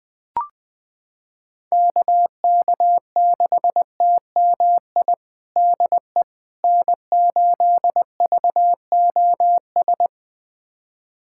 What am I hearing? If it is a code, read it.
Morse code, heard as a keyed tone, about 21 wpm, 700 Hz. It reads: KK6TMI DE N8VOS
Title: USA Callsign DE Callsign 20wpm